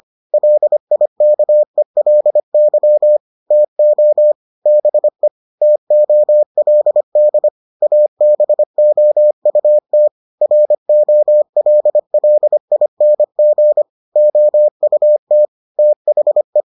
Morse code, heard as a keyed tone, about 25 wpm, 600 Hz. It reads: LIKELY TO BE TOLD ABOUT ROLLING OUT THE